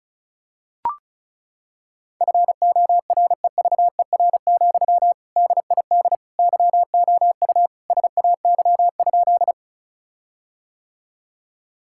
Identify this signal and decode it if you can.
Morse code, heard as a keyed tone, about 36 wpm, 700 Hz. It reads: FOREVER, DID YOU SAY?
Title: Sentences from Top 1600 Words 35wpm